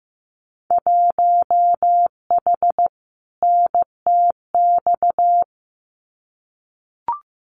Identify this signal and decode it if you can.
Morse code, heard as a keyed tone, about 15 wpm, 700 Hz. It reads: 1H NTX
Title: Winter Field Day Exchanges 15wpm